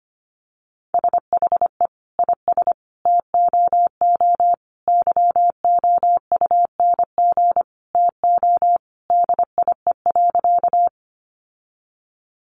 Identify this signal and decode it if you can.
Morse code, heard as a keyed tone, about 25 wpm, 700 Hz. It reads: SHE IS TOO YOUNG TO DIE.
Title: Sentences from Top 1300 Words 25wpm